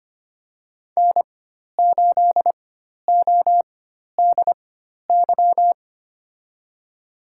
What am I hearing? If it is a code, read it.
Morse code, heard as a keyed tone, about 25 wpm, spaced as for 12 wpm, 700 Hz. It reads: N8ODY
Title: USA Call Signs - 5 Characters - 4x Character Spacing - 25wpm